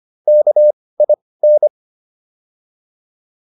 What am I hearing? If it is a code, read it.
Morse code, heard as a keyed tone, about 25 wpm, spaced as for 18 wpm, 600 Hz. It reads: KIN